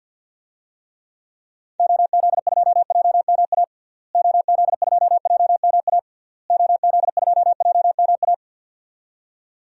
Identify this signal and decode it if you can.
Morse code, heard as a keyed tone, about 52 wpm, 700 Hz. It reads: OZ2JMA OZ2JMA OZ2JMA
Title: DXCC-100 Call Signs - 6 Characters - Repeated 3x - 50wpm